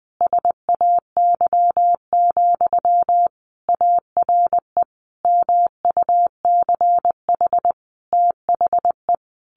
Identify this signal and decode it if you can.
Morse code, heard as a keyed tone, about 20 wpm, 700 Hz. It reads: SAY, ARE MUCH THE
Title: Sentences from Top 1100 Words 20wpm